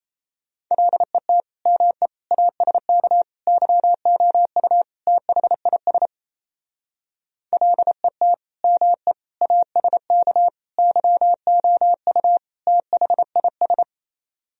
Morse code, heard as a keyed tone, about 35 wpm, 700 Hz. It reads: LET ME ASK YOU THIS LET ME ASK YOU THIS
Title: Speed-Racing - Sets of 5 Words 28wpm